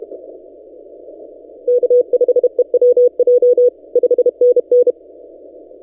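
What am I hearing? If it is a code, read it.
Morse code, heard as a keyed tone, about 32 wpm, 490 Hz. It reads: K5EWJ 5NN
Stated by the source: Morse from DXpedition station A35X on 12 m, posted by N6ML.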